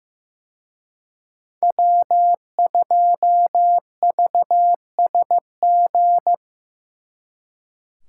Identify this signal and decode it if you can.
Morse code, heard as a keyed tone, about 18 wpm, 700 Hz. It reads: W2VSG
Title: Speed Racing - USA Call Signs 15wpm